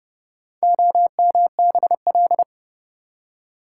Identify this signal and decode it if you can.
Morse code, heard as a keyed tone, about 28 wpm, 700 Hz. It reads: OMBL